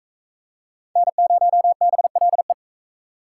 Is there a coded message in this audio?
Yes: Morse code from 1.0 to 2.5 s, heard as a keyed tone, about 39 wpm, 700 Hz. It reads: N0BLE